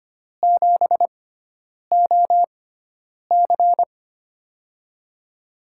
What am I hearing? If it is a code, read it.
Morse code, heard as a keyed tone, about 25 wpm, spaced as for 9 wpm, 700 Hz. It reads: 7OC
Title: USA Call Signs - 4 Characters - 6x Character Spacing - 25wpm